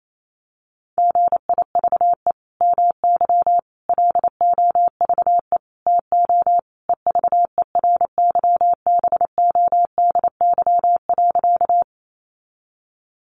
Morse code, heard as a keyed tone, about 28 wpm, 700 Hz. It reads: GIVE MY LOVE TO EVERYBODY.
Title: Sentences from Top 1400 Words 28wpm